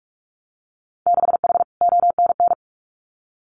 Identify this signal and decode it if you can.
Morse code, heard as a keyed tone, about 46 wpm, 700 Hz. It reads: 6H ONN
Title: Winter Field Day Exchanges Encore 45wpm